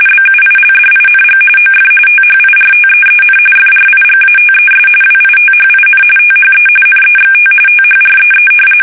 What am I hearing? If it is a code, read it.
RTTY, about 45.45 baud. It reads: BUILDING OF THIRTEEN FRIGATES WITHIN THE NEXT THREE
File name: fsk850 / usnavy1